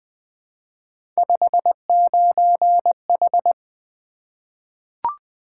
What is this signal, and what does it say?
Morse code, heard as a keyed tone, about 20 wpm, 700 Hz. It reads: 59H
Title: Three Letter-Number Combinations - 20wpm